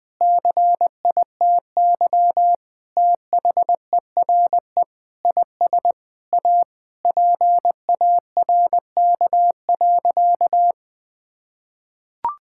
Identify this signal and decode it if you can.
Morse code, heard as a keyed tone, about 20 wpm, 700 Hz. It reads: CITY THERE IS A PARK.
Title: Sentences with 5-Word Phrases - Spoken Beginning - 20wpm